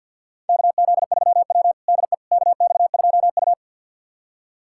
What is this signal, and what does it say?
Morse code, heard as a keyed tone, about 49 wpm, 700 Hz. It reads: KZ3W DE KX2U
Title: USA Callsign DE Callsign - 4 Characters - 50wpm